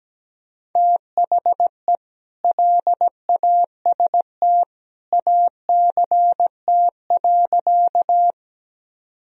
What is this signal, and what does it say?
Morse code, heard as a keyed tone, about 17 wpm, 700 Hz. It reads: THE LAST ACT.
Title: Sentences from Top 1000 Words - Review Part 1 - 17wpm